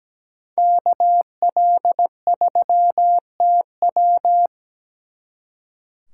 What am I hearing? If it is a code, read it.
Morse code, heard as a keyed tone, about 20 wpm, 700 Hz. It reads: KL3TW
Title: Speed Racing - USA 2x2 Call Signs 17wpm